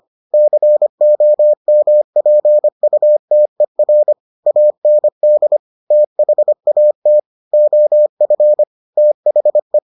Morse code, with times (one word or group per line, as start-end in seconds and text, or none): 0.3-4.1 COMPUTER
4.5-5.6 AND
5.9-7.2 THAT
7.5-8.6 OF
9.0-9.8 THE